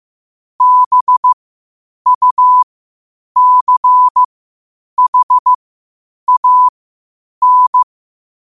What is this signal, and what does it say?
Morse code, heard as a keyed tone, about 15 wpm, spaced as for 8 wpm, 1000 Hz. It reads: BUCHAN
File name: qotdfiles / 8wpm